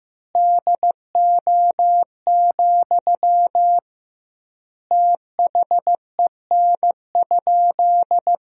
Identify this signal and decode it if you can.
Morse code, heard as a keyed tone, about 15 wpm, 700 Hz. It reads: DO, THEN?
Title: 2x Word-Spacing - Sentences from Top 100 Words - 15wpm